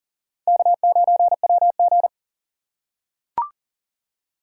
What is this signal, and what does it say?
Morse code, heard as a keyed tone, about 40 wpm, 700 Hz. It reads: K9WG
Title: USA Call Signs - 4 Characters - 4x Character Spacing - 40wpm